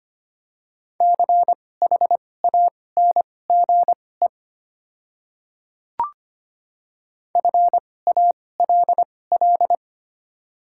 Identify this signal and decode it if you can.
Morse code, heard as a keyed tone, about 25 wpm, spaced as for 18 wpm, 700 Hz. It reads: CHANGE FALL
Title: 2x Character Spacing - Top 500 Words - 25wpm